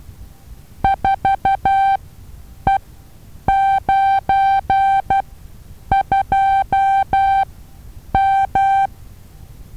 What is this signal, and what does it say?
Morse code, heard as a keyed tone, about 12 wpm, spaced as for 8 wpm, 780 Hz. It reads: 4E92M
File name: Taviro HA6VA B